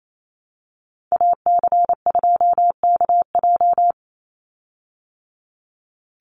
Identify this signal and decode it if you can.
Morse code, heard as a keyed tone, about 28 wpm, 700 Hz. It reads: AC2KJ
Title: Call Signs Repeated 2x - 28wpm